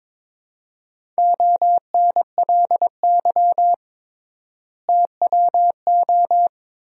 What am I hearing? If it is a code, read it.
Morse code, heard as a keyed tone, about 22 wpm, 700 Hz. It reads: ONLY TWO
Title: 3x Word-Spacing - Sentences from Top 200 Words - 22wpm